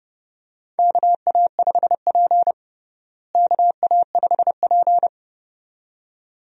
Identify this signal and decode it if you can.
Morse code, heard as a keyed tone, about 30 wpm, 700 Hz. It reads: KA5P KA5P
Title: USA Call Signs - 4 Characters - Repeated 2x - 30wpm